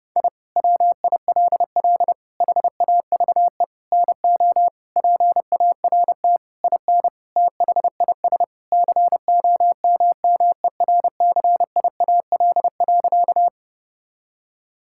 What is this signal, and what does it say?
Morse code, heard as a keyed tone, about 30 wpm, 700 Hz. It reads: I WILL HAVE NO PART IN THIS COMMERCIAL.
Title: Sentences from Top 1000 Words - Review Part 5 - 30wpm